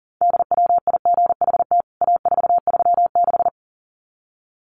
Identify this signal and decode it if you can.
Morse code, heard as a keyed tone, about 40 wpm, 700 Hz. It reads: DWIGHT A436
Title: CWT Exchange 40wpm